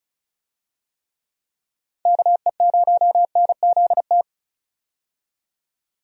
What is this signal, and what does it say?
Morse code, heard as a keyed tone, about 34 wpm, 700 Hz. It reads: KE0NZT
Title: USA Call Signs - 6 Characters - 35wpm